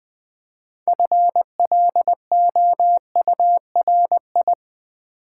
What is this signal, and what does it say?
Morse code, heard as a keyed tone, about 20 wpm, 700 Hz. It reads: FLOURI
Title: Build A Word - 2 Letters At A Time - 8-Letter Words - 20wpm